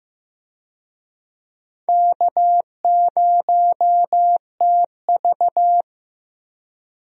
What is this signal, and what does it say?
Morse code, heard as a keyed tone, about 15 wpm, 700 Hz. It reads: K0TV
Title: USA Call Signs - 4 Characters - Repeated 3x - 15wpm